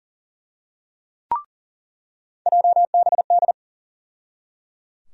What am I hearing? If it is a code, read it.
Morse code, heard as a keyed tone, about 37 wpm, 700 Hz. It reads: JBD
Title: USA Call Sign Postfixes - 3 Letters 40wpm